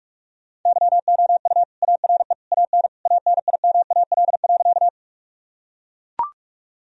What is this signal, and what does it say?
Morse code, heard as a keyed tone, about 42 wpm, 700 Hz. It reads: YOU ARE AN ANIMAL.
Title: Sentences from Top 1000 Words - Review Part 1 - 45wpm